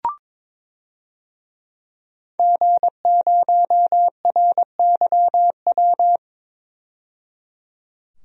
Morse code, heard as a keyed tone, about 22 wpm, 700 Hz. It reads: G0RYW
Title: DXCC-100 Call Signs - 5 Characters - 22wpm